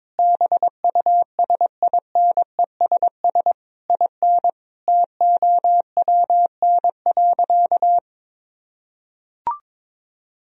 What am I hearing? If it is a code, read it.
Morse code, heard as a keyed tone, about 21 wpm, 700 Hz. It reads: BUSINESS IN TOWN.
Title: Sentences from Top 1000 Words - Review Part 2 - 22wpm